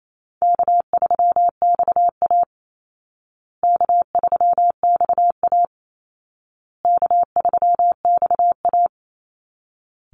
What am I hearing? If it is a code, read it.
Morse code, heard as a keyed tone, about 28 wpm, 700 Hz. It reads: K3XA K3XA K3XA
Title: Call Signs Repeated 2x - 28wpm